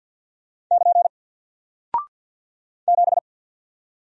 Morse code, heard as a keyed tone, about 48 wpm, 700 Hz. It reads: <KN> 7
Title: Single Character (Letter:Number:Punctuation:Prosign) 50wpm